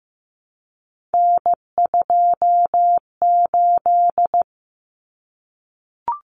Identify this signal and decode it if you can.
Morse code, heard as a keyed tone, about 15 wpm, 700 Hz. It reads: N28
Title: Common Cut Numbers - 3-Digit 15wpm